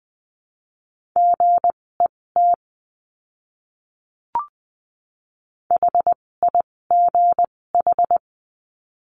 Morse code, heard as a keyed tone, about 20 wpm, spaced as for 16 wpm, 700 Hz. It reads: GET HIGH
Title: 256 Lesson 42a - Words 12-20wpm